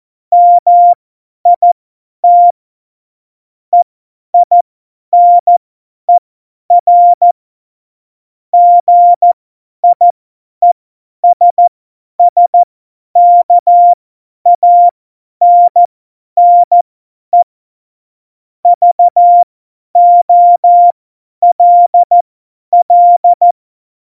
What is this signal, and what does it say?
Morse code, heard as a keyed tone, about 14 wpm, spaced as for 10 wpm, 700 Hz. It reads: MIT EINER GIESSKANNE VOLL
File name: Morsekurs / Witze / Witz 17. November 2020, 50 F70